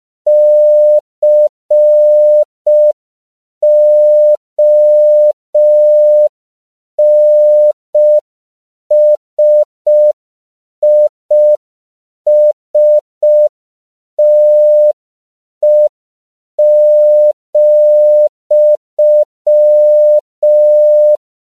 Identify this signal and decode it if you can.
Morse code, heard as a keyed tone, about 5 wpm, 600 Hz. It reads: CONSISTE,
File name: CW / 5PPM